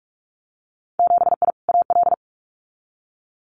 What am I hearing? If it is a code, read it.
Morse code, heard as a keyed tone, about 44 wpm, 700 Hz. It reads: 7I AL
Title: Winter Field Day Exchanges Encore 45wpm